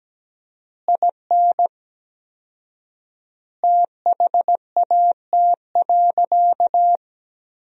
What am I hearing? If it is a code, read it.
Morse code, heard as a keyed tone, about 17 wpm, 700 Hz. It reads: IN THAT.
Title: 4x Word-Spacing - Sentences from Top 200 Words - 17wpm